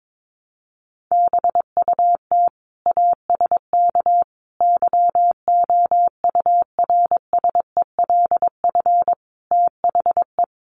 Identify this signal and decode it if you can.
Morse code, heard as a keyed tone, about 22 wpm, 700 Hz. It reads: BUT ASK YOURSELF THE
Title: Sentences from Top 600 Words 22wpm